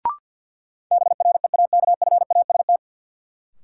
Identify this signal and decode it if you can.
Morse code, heard as a keyed tone, about 52 wpm, 700 Hz. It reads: BREAKFAST
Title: Sentences from Top 1500 Words - New Word Drill Alt - 50wpm